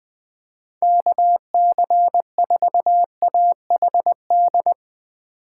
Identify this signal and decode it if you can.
Morse code, heard as a keyed tone, about 20 wpm, 700 Hz. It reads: KC4AHD
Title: USA Call Signs - 6 Characters - 6x Character Spacing - 20wpm